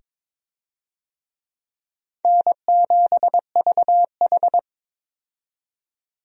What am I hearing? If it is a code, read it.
Morse code, heard as a keyed tone, about 21 wpm, 700 Hz. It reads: N7VH
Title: USA Call Signs - 4 Characters - 22wpm